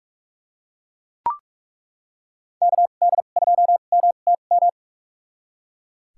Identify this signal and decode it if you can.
Morse code, heard as a keyed tone, about 41 wpm, spaced as for 31 wpm, 700 Hz. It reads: KD2MTM
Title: USA Call Signs - 6 Characters - 2x Character Spacing - 45wpm